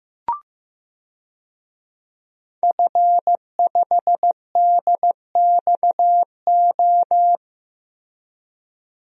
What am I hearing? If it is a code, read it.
Morse code, heard as a keyed tone, about 15 wpm, 700 Hz. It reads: F5DXO
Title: DXCC-100 Call Signs - 5 Characters - Repeated 3x - 15wpm